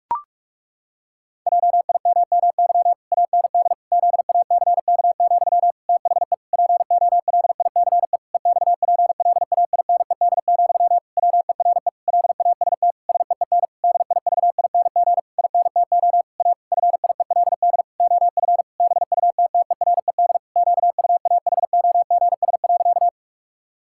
Morse code, heard as a keyed tone, about 46 wpm, 700 Hz. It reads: JIMMY AND ZACK, THE POLICE EXPLAINED, WERE LAST SEEN DIVING INTO A FIELD OF BUTTERED QUAHOGS.
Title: Pangram Challenge v2 - 45wpm